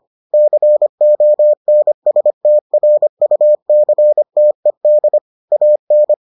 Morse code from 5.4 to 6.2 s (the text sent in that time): AN